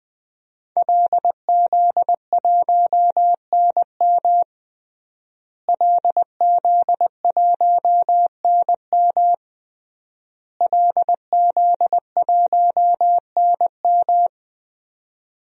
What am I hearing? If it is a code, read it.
Morse code, heard as a keyed tone, about 20 wpm, 700 Hz. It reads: LZ1NM LZ1NM LZ1NM
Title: DXCC-100 Call Signs - 5 Characters - Repeated 3x - 20wpm